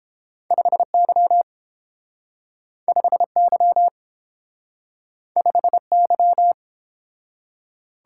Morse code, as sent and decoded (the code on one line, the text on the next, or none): ..... -.-- / ..... -.-- / ..... -.--
5Y 5Y 5Y